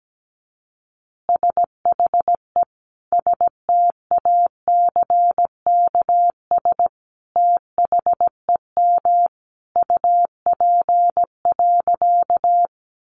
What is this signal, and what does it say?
Morse code, heard as a keyed tone, about 17 wpm, 700 Hz. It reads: SHE STACKS THEM UP.
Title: Easy Phrases 17wpm